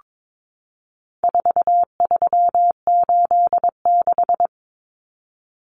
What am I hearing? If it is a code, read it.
Morse code, heard as a keyed tone, about 22 wpm, 700 Hz. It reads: 4386
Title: Numbers - 4-Digit 22wpm